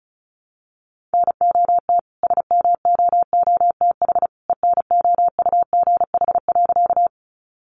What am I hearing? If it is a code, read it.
Morse code, heard as a keyed tone, about 34 wpm, 700 Hz. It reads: NOT SMOOTH ENOUGH.
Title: Sentences from Top 1400 Words 35wpm